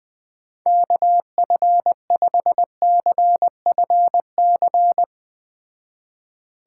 Morse code, heard as a keyed tone, about 20 wpm, 700 Hz. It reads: KF5CFC
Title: Speed Racing - USA 2x3 Call Signs 17wpm